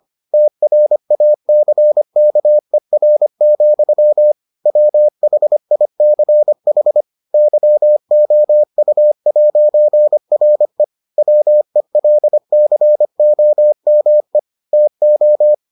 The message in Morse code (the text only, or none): TRACKER, WHICH YOU'RE WELCOME TO